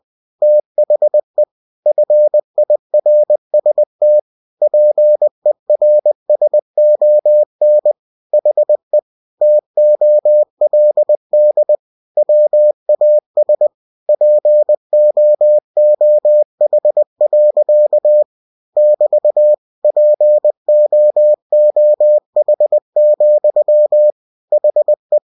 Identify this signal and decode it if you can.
Morse code, heard as a keyed tone, about 20 wpm, 600 Hz. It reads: THE FIRST PERSON HE TOLD WAS POOH. <BT> POOH, HE
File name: cw-books / winnie-the-pooh / winnie-20wpm-page-0102